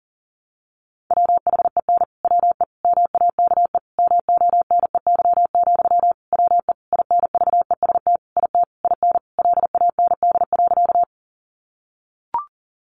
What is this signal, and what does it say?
Morse code, heard as a keyed tone, about 40 wpm, 700 Hz. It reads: WHEN WE MAKE MONEY, WE INVEST IT IN LAND.
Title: Sentences from Top 1100 Words 40wpm